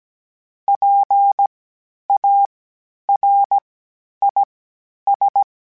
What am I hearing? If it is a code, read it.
Morse code, heard as a keyed tone, about 17 wpm, spaced as for 10 wpm, 800 Hz. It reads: PARIS
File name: PARIS Calibration / PARIS mp3s / PARIS@10-@16